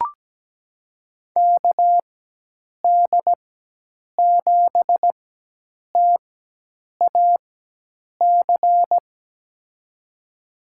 Morse code, heard as a keyed tone, about 17 wpm, spaced as for 8 wpm, 700 Hz. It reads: KD7TAC
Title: USA Call Signs - 4x Character Spacing - 17wpm